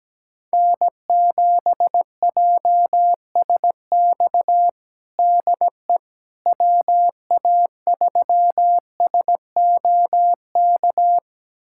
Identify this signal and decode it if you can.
Morse code, heard as a keyed tone, about 17 wpm, 700 Hz. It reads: N7JSX DE WA3SOK
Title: USA Callsign DE Callsign 17wpm